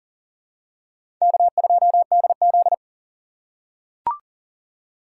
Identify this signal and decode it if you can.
Morse code, heard as a keyed tone, about 38 wpm, 700 Hz. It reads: K2DZ
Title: USA Call Signs - 4 Characters - 4x Character Spacing - 40wpm